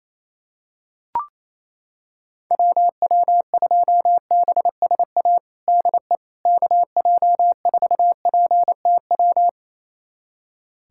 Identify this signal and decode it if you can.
Morse code, heard as a keyed tone, about 28 wpm, 700 Hz. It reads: WW2BSA DE KJ4PTW
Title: USA Callsign DE Callsign - 6 Characters - 28wpm